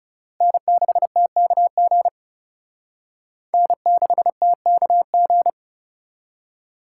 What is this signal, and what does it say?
Morse code, heard as a keyed tone, about 37 wpm, 700 Hz. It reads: N6TKG N6TKG
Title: Speed Racing - USA 1x3 Call Signs 30wpm